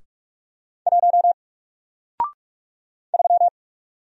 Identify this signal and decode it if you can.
Morse code, heard as a keyed tone, about 47 wpm, 700 Hz. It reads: 1 3
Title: Single Character (Letter:Number:Punctuation:Prosign) Alt 45wpm